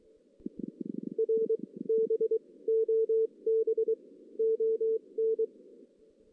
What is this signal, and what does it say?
Morse code, heard as a keyed tone, about 23 wpm, spaced as for 17 wpm, 450 Hz. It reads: R BOBON
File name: 2016 JAN CW NAQP / 2016 Jan naqp k5ot - 0-3